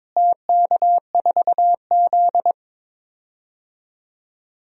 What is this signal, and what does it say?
Morse code, heard as a keyed tone, about 23 wpm, 700 Hz. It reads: TK4Z
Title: DXCC-100 Call Signs - 4 Characters - Repeated 2x - 22wpm